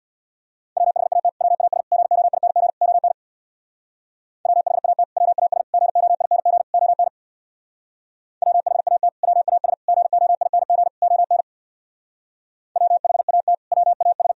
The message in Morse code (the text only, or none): WHAT WAS GOING ON WHAT WAS GOING ON WHAT WAS GOING ON WHAT WAS